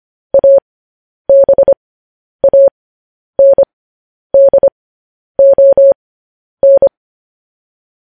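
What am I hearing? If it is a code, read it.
Morse code, heard as a keyed tone, about 25 wpm, spaced as for 10 wpm, 550 Hz. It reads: ABANDON
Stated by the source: Morse code news headlines sent at 10 wpm.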